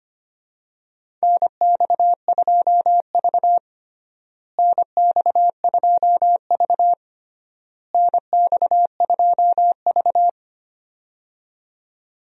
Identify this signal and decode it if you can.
Morse code, heard as a keyed tone, about 25 wpm, 700 Hz. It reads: NX2V NX2V NX2V
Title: USA Call Signs - 4 Characters - Repeated 3x - 25wpm